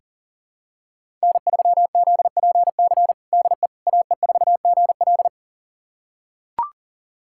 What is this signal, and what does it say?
Morse code, heard as a keyed tone, about 40 wpm, 700 Hz. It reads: N3ZPC DE AE4GL
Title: USA Callsign DE Callsign - 5 Characters - 40wpm